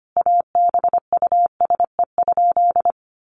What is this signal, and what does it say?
Morse code, heard as a keyed tone, about 25 wpm, 700 Hz. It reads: ABUSE?
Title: Sentences from Top 1300 Words 25wpm